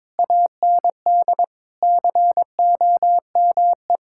Morse code, heard as a keyed tone, about 22 wpm, 700 Hz. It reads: AND COME
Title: Sentences from Top 1000 Words - Review Part 5 - 22wpm